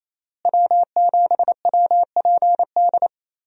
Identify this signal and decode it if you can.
Morse code, heard as a keyed tone, about 27 wpm, 700 Hz. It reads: W7WPD
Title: USA Call Signs - 5 Characters - Repeated 2x - 28wpm